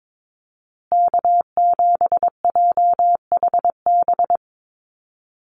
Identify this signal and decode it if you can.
Morse code, heard as a keyed tone, about 22 wpm, 700 Hz. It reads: K7JHB
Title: USA Call Signs Repeated 2x 22wpm